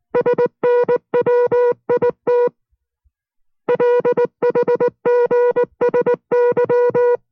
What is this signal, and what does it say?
Morse code, heard as a keyed tone, about 19 wpm, 490 Hz. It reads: SNWIT LHGSY